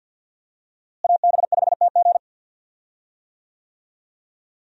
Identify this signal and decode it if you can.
Morse code, heard as a keyed tone, about 53 wpm, 700 Hz. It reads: AB5TG
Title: USA Call Signs - 5 Characters - Repeated 3x - 50wpm